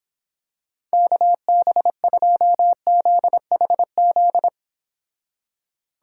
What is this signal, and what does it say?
Morse code, heard as a keyed tone, about 26 wpm, 700 Hz. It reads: KB2ZHZ